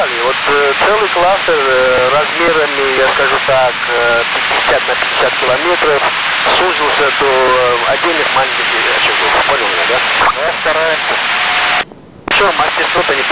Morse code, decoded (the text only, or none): none